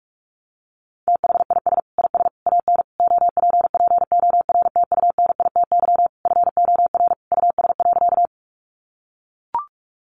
Morse code, heard as a keyed tone, about 44 wpm, 700 Hz. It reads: THIS IS AN OPPORTUNITY FOR US.
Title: Sentences from Top 1000 Words 45wpm